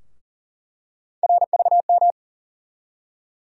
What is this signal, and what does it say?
Morse code, heard as a keyed tone, about 40 wpm, 700 Hz. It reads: RVM
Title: USA Call Sign Postfixes - 3 Letters 40wpm